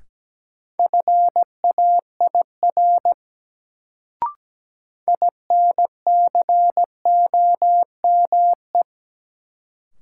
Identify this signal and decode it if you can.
Morse code, heard as a keyed tone, about 17 wpm, 700 Hz. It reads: FAIR INCOME
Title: Sentences from Top 1000 Words - Review Part 5 - New Word Drill - 17wpm